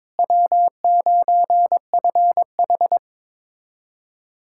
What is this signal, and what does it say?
Morse code, heard as a keyed tone, about 22 wpm, 700 Hz. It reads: W9FH
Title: USA Callsign DE Callsign - 4 Characters - 22wpm